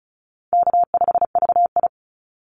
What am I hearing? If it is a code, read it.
Morse code, heard as a keyed tone, about 34 wpm, 700 Hz. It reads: K5VI
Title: Easy Call Signs Repeated 2x - 35wpm